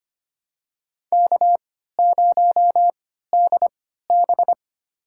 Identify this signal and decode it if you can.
Morse code, heard as a keyed tone, about 25 wpm, spaced as for 14 wpm, 700 Hz. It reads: K0DB
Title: USA Call Signs - 4 Characters - 3x Character Spacing - 25wpm